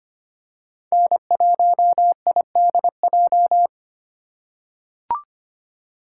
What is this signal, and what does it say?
Morse code, heard as a keyed tone, about 25 wpm, 700 Hz. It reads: N1IDJ
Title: USA Call Signs - 3x Character Spacing - 25wpm